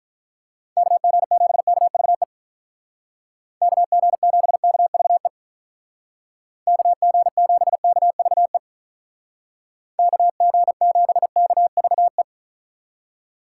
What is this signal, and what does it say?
Morse code, heard as a keyed tone, about 43 wpm, 700 Hz. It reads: KG7KVE KG7KVE KG7KVE KG7KVE
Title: Speed Racing - USA 2x3 Call Signs 35wpm